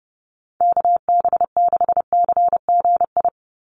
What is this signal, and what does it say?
Morse code, heard as a keyed tone, about 28 wpm, 700 Hz. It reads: KB6CGI
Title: USA Call Signs 30wpm